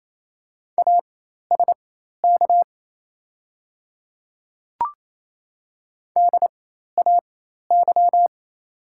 Morse code, heard as a keyed tone, about 28 wpm, spaced as for 13 wpm, 700 Hz. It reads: ASK DAY
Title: 4x Character Spacing - Top 200 Words - 28wpm